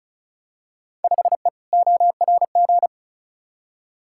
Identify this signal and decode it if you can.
Morse code, heard as a keyed tone, about 36 wpm, 700 Hz. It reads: 5E ORG
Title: Field Day Exchanges Encore - 35wpm